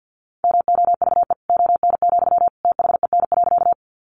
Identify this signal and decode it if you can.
Morse code, heard as a keyed tone, about 50 wpm, 700 Hz. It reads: MOVE ON, THEN.
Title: Sentences from Top 200 Words 50wpm